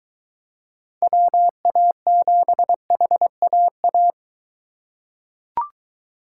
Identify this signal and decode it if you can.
Morse code, heard as a keyed tone, about 21 wpm, 700 Hz. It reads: WA7HAA